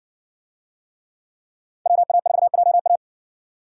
Morse code, heard as a keyed tone, about 60 wpm, 700 Hz. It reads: WA4JA